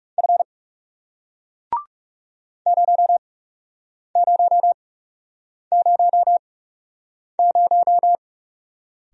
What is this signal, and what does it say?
Morse code, heard as a keyed tone, about 40 wpm, spaced as for 8 wpm, 700 Hz. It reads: F 0000